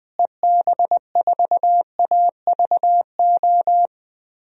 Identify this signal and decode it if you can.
Morse code, heard as a keyed tone, about 20 wpm, 700 Hz. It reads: EB4AVO
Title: DXCC-100 Call Signs - 6 Characters - Repeated 3x - 20wpm